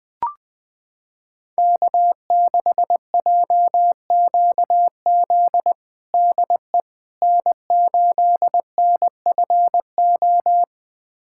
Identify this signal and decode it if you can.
Morse code, heard as a keyed tone, about 20 wpm, 700 Hz. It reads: K6JQZ DE N8NFO
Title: USA Callsign DE Callsign - 5 Characters - 20wpm